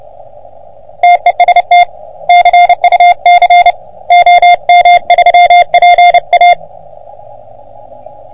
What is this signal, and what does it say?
Morse code, heard as a keyed tone, about 28 wpm, 720 Hz. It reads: TEST CUC OM3PA
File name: OM3PA PILUP CUC 388